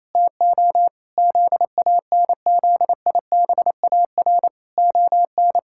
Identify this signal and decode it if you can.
Morse code, heard as a keyed tone, about 28 wpm, 700 Hz. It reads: TO ZANZIBAR ON
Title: Pangram Challenge v2 - 28wpm